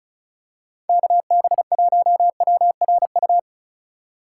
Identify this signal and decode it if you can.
Morse code, heard as a keyed tone, about 36 wpm, 700 Hz. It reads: KB1WRU
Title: USA Call Signs - 6 Characters - 6x Character Spacing - 35wpm